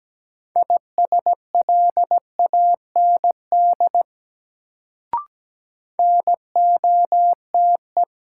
Morse code, heard as a keyed tone, about 17 wpm, 700 Hz. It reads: ISLAND NOTE
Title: Sentences from Top 500 Words - New Word Drill - 17wpm